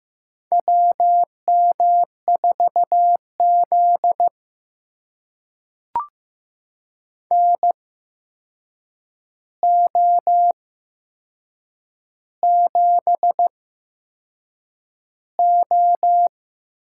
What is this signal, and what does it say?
Morse code, heard as a keyed tone, about 15 wpm, spaced as for 6 wpm, 700 Hz. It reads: WM4Z NO7O
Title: USA Call Signs - 4 Characters - 8x Character Spacing - 15wpm